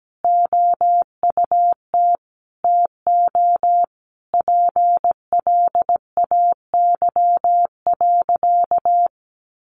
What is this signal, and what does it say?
Morse code, heard as a keyed tone, about 17 wpm, 700 Hz. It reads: OUT TO PLAY.
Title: Easy Phrases 17wpm